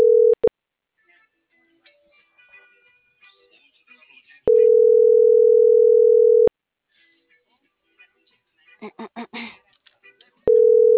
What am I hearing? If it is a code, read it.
Ringback tone (North America), 440 + 480 Hz.